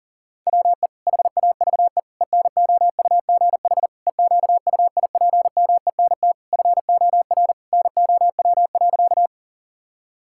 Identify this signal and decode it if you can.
Morse code, heard as a keyed tone, about 40 wpm, 700 Hz. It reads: WE HAVE ENOUGH EQUIPMENT FOR NOW.
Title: Sentences from Top 1000 Words - Review Part 5 - 40wpm